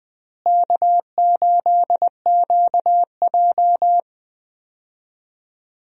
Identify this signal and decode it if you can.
Morse code, heard as a keyed tone, about 21 wpm, 700 Hz. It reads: K8QJ
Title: USA Call Signs - 4 Characters - Repeated 3x - 20wpm